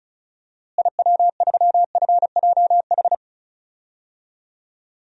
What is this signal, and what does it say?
Morse code, heard as a keyed tone, about 36 wpm, 700 Hz. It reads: IW3FJH
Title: DXCC-100 Call Signs - 6 Characters - Repeated 3x - 35wpm